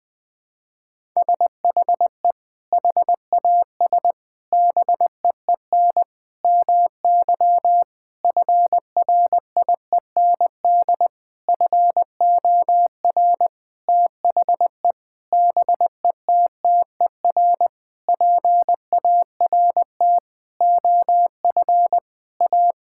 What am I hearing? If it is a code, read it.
Morse code, heard as a keyed tone, about 20 wpm, 700 Hz. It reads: SHE HAS BEEN MY FRIEND FOR THE BETTER PART OF A
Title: Sentences with 6-Word Phrases - 20wpm